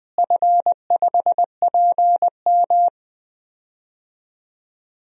Morse code, heard as a keyed tone, about 20 wpm, 700 Hz. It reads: F5PM
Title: DXCC-100 Call Signs - 4 Characters - Repeated 3x - 20wpm